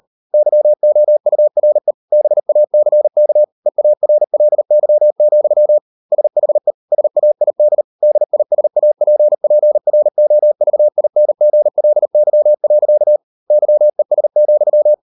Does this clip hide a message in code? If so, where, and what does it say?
Yes: Morse code from 0.3 to 14.9 s, heard as a keyed tone, about 38 wpm, 600 Hz. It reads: YOURE BACK EARLY, SHE SAID DISAPPROVINGLY. YES,